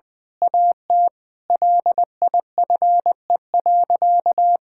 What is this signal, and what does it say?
Morse code, heard as a keyed tone, about 20 wpm, 700 Hz. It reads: AT LIFE.
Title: Difficult Sentences with Hints - 20wpm